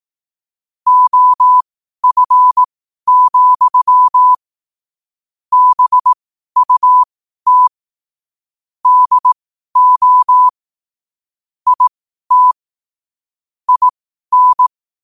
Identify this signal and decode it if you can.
Morse code, heard as a keyed tone, about 18 wpm, spaced as for 12 wpm, 1000 Hz. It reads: OF, BUT DO IT IN
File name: qotdfiles / 12wpm